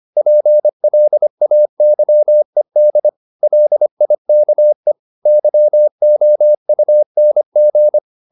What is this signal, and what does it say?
Morse code, heard as a keyed tone, about 25 wpm, 600 Hz. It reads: PLAYED LIKE YOUNG
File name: the old man and the sea part 1 25wpm